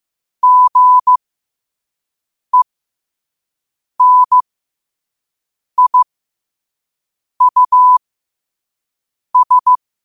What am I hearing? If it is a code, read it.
Morse code, heard as a keyed tone, about 15 wpm, spaced as for 5 wpm, 1000 Hz. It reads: GENIUS
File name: qotdfiles / 5wpm